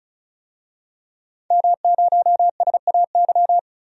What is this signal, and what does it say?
Morse code, heard as a keyed tone, about 34 wpm, 700 Hz. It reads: M0SAY
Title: DXCC-100 Call Signs - 5 Characters - 35wpm